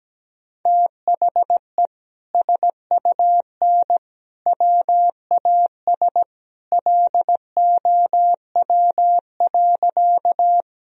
Morse code, heard as a keyed tone, about 17 wpm, 700 Hz. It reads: THE SUN WAS LOW.
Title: Sentences from Top 1000 Words - Review Part 1 - 17wpm